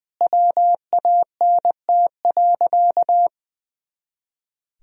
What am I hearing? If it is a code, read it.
Morse code, heard as a keyed tone, about 20 wpm, 700 Hz. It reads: WANT.
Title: Sentences from Top 200 Words and Abbreviations 20wpm